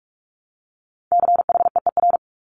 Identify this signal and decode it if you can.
Morse code, heard as a keyed tone, about 46 wpm, 700 Hz. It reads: CHEER
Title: Multi-letter Phonemes Repeated 3x 45wpm